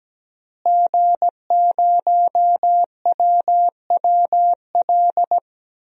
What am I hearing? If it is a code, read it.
Morse code, heard as a keyed tone, about 17 wpm, 700 Hz. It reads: G0WWL
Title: DXCC-100 Call Signs - 5 Characters - Repeated 3x - 17wpm